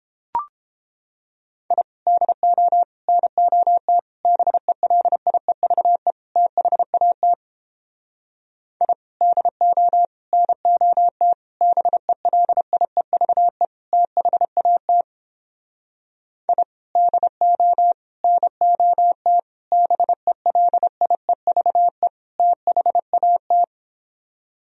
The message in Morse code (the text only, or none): I DO NOT BELIEVE THAT I DO NOT BELIEVE THAT I DO NOT BELIEVE THAT